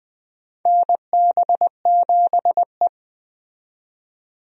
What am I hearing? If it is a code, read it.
Morse code, heard as a keyed tone, about 21 wpm, 700 Hz. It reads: NB7E
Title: USA Call Signs - 4 Characters - 20wpm